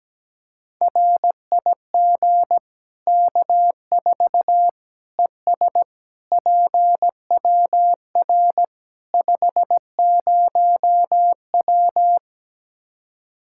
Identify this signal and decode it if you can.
Morse code, heard as a keyed tone, about 17 wpm, 700 Hz. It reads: RIG K4 ES PWR 50W
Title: K1USN SST QSO Practice 17wpm